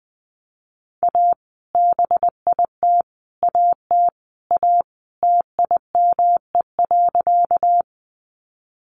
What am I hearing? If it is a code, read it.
Morse code, heard as a keyed tone, about 20 wpm, 700 Hz. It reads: A BIT AT A TIME.
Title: Sentences from Top 900 Words 20wpm